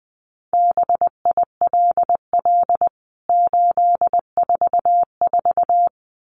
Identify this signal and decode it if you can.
Morse code, heard as a keyed tone, about 20 wpm, 700 Hz. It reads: BILL 844
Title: CWT Exchange 20wpm